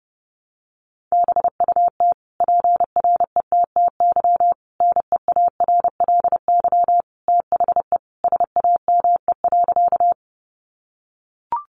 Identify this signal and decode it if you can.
Morse code, heard as a keyed tone, about 28 wpm, 700 Hz. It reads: BUT PRETTY NEARLY THE SAME.
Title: Sentences from Top 1300 Words 30wpm